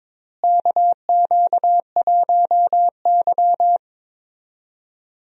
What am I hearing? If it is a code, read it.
Morse code, heard as a keyed tone, about 22 wpm, 700 Hz. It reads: KQ1Y
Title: USA Call Signs - 4 Characters - Repeated 2x - 22wpm